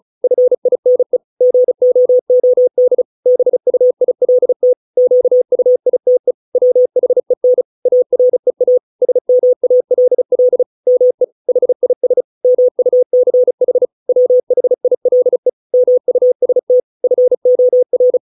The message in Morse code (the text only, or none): FINE GOOD BUILT QUITE WHEN AREA SMALL ME HIS MUCH WHILE MUST FOR